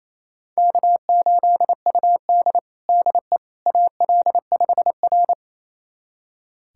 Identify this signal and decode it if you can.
Morse code, heard as a keyed tone, about 28 wpm, 700 Hz. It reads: K8UD DE AL5R
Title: USA Callsign DE Callsign - 4 Characters - 28wpm